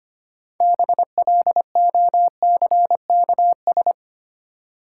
Morse code, heard as a keyed tone, about 25 wpm, 690 Hz. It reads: BLOCKS
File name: practice / 25wpm / Easy Words with Hints - 25wpm